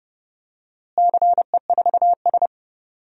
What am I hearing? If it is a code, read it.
Morse code, heard as a keyed tone, about 30 wpm, 700 Hz. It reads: CE4S
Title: DXCC-100 Call Signs - 4 Characters - Repeated 2x - 30wpm